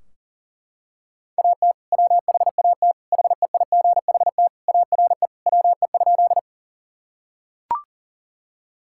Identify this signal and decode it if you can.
Morse code, heard as a keyed tone, about 40 wpm, 700 Hz. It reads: AT WHAT HEIGHT ARE WE?
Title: Sentences from Top 1600 Words 40wpm